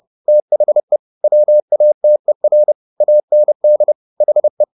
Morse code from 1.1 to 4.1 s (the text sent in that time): WATER AND